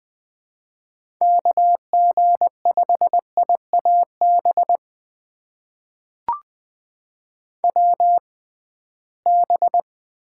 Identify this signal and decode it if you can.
Morse code, heard as a keyed tone, about 20 wpm, spaced as for 15 wpm, 700 Hz. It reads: KG5IAB WB
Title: USA Call Signs - 6 Characters - 6x Character Spacing - 20wpm